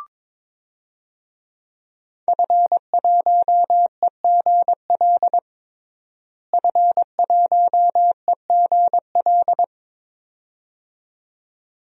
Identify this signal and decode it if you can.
Morse code, heard as a keyed tone, about 22 wpm, 700 Hz. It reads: F1EGL F1EGL
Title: DXCC-100 Call Signs - 5 Characters - Repeated 2x - 22wpm